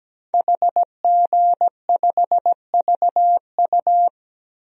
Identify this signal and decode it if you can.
Morse code, heard as a keyed tone, about 17 wpm, 700 Hz. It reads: HG5VU
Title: DXCC-100 Call Signs - 5 Characters - Repeated 3x - 17wpm